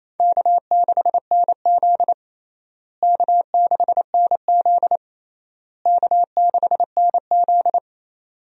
Morse code, heard as a keyed tone, about 28 wpm, 700 Hz. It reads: K6NZ K6NZ K6NZ
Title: USA Call Signs - 4 Characters - Repeated 3x - 28wpm